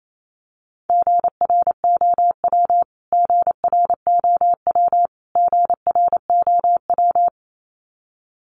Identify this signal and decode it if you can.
Morse code, heard as a keyed tone, about 28 wpm, 700 Hz. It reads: GROW GROW GROW
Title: Top 500 Words Repeated 28wpm